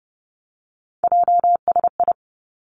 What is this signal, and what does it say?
Morse code, heard as a keyed tone, about 30 wpm, 700 Hz. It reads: JSI
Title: Three Challenging Letter-Number Combinations 30wpm